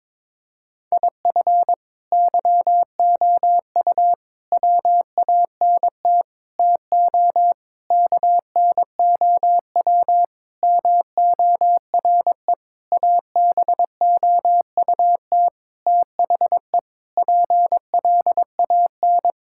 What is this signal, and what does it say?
Morse code, heard as a keyed tone, about 22 wpm, 700 Hz. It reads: IF YOU WANT TO KNOW MORE ABOUT THE PLAN
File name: practice / 22wpm / Sentences with 8-Word Phrases - 22wpm